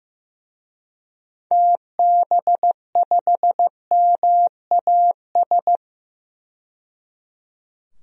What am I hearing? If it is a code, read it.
Morse code, heard as a keyed tone, about 15 wpm, 700 Hz. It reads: TB5MAS
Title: DXCC-100 Call Signs - 6 Characters - 15wpm